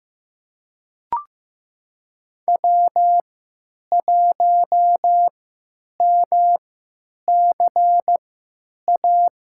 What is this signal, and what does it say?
Morse code, heard as a keyed tone, about 15 wpm, spaced as for 9 wpm, 700 Hz. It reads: W1MCA
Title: USA Call Signs - 5 Characters - 3x Character Spacing - 15wpm